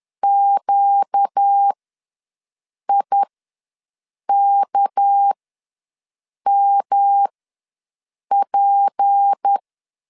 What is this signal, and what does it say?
Morse code, heard as a keyed tone, about 10 wpm, spaced as for 6 wpm, 800 Hz. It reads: QIKMP
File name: LEZIONI / 6 PKX0 / 6 PK ws 10wpm 3745